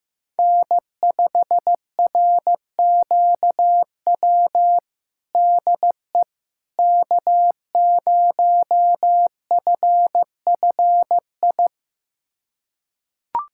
Morse code, heard as a keyed tone, about 15 wpm, 700 Hz. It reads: N5RQW DE K0FFI
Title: USA Callsign DE Callsign 15wpm